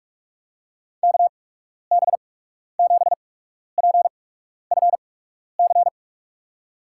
Morse code, heard as a keyed tone, about 44 wpm, spaced as for 12 wpm, 700 Hz. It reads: KB7PFC
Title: USA Call Signs - 6 Characters - 8x Character Spacing - 45wpm